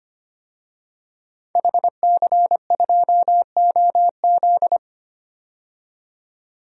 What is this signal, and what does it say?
Morse code, heard as a keyed tone, about 25 wpm, 700 Hz. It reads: HC2OZ
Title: DXCC-100 Call Signs - 5 Characters - 25wpm